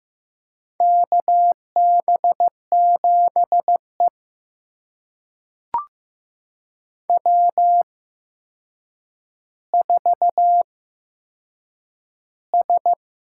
Morse code, heard as a keyed tone, about 15 wpm, spaced as for 7 wpm, 700 Hz. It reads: KB7E W4S
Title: USA Call Signs - 4 Characters - 8x Character Spacing - 15wpm